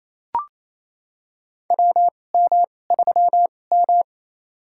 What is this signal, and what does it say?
Morse code, heard as a keyed tone, about 27 wpm, spaced as for 20 wpm, 700 Hz. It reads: WM3M
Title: USA Call Signs - 4 Characters - 2x Character Spacing - 28wpm